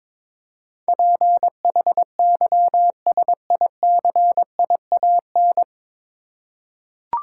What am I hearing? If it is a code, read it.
Morse code, heard as a keyed tone, about 22 wpm, 700 Hz. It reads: PHYSICIAN
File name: 9 Letter Words - First 3 Letters Given - 22wpm